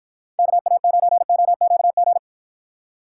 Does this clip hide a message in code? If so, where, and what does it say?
Yes: Morse code from 0.4 to 2.2 s, heard as a keyed tone, about 52 wpm, 700 Hz. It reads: KA9OQG